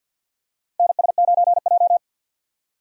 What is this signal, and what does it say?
Morse code, heard as a keyed tone, about 46 wpm, 700 Hz. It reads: NS9J
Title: USA Call Signs - 4 Characters - 50wpm